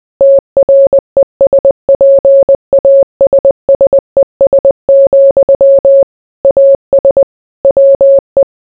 Morse code, heard as a keyed tone, about 20 wpm, 550 Hz. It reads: TRESPASSES, AS WE